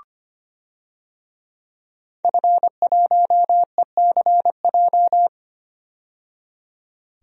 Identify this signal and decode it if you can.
Morse code, heard as a keyed tone, about 25 wpm, 700 Hz. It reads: F1ECJ
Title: DXCC-100 Call Signs - 5 Characters - 25wpm